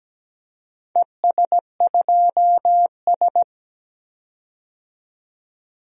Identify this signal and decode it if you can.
Morse code, heard as a keyed tone, about 17 wpm, 700 Hz. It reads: ES2S
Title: DXCC-100 Call Signs - 4 Characters - 17wpm